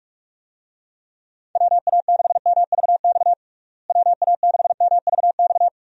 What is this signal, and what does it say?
Morse code, heard as a keyed tone, about 42 wpm, 700 Hz. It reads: WA6MVX WA6MVX
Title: USA Call Signs - 6 Characters - Repeated 2x - 45wpm